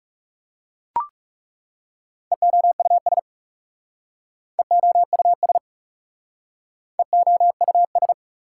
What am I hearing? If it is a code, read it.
Morse code, heard as a keyed tone, about 40 wpm, 700 Hz. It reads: EOUS EOUS EOUS